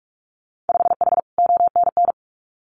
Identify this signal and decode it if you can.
Morse code, heard as a keyed tone, about 46 wpm, 700 Hz. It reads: 5H ONN
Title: Winter Field Day Exchanges Encore 45wpm